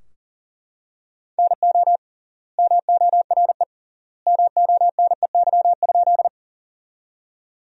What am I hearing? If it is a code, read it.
Morse code, heard as a keyed tone, about 40 wpm, 700 Hz. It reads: NO MORE MONEY?
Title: 3x Word-Spacing - Sentences from Top 400 Words - 40wpm